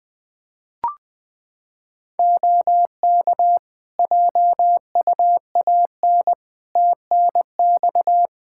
Morse code, heard as a keyed tone, about 21 wpm, 700 Hz. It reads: OK JUAN TNX
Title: K1USN SST QSO Practice 20wpm